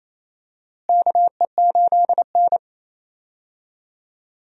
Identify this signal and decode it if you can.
Morse code, heard as a keyed tone, about 28 wpm, 700 Hz. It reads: KE8N
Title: USA Call Signs - 4 Characters - Repeated 3x - 28wpm